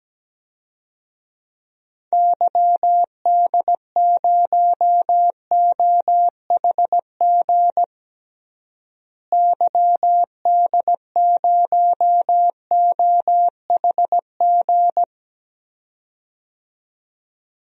Morse code, heard as a keyed tone, about 17 wpm, 700 Hz. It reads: YD0OHG YD0OHG
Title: DXCC-100 Call Signs - 6 Characters - Repeated 2x - 17wpm